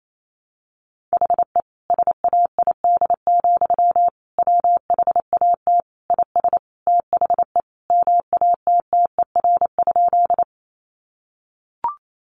Morse code, heard as a keyed tone, about 28 wpm, 700 Hz. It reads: HE SAID, WHAT IS THE MATTER?
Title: Sentences from Top 700 Words 28wpm